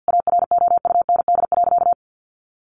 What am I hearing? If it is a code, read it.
Morse code, heard as a keyed tone, about 50 wpm, 700 Hz. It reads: AROUND.
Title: Sentences from Top 800 Words 50wpm